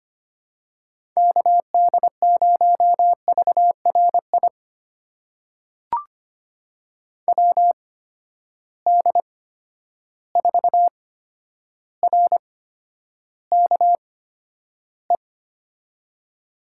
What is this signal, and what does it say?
Morse code, heard as a keyed tone, about 25 wpm, spaced as for 11 wpm, 700 Hz. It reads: KD0VRI WD4RKE
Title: USA Call Signs - 6 Characters - 8x Character Spacing - 25wpm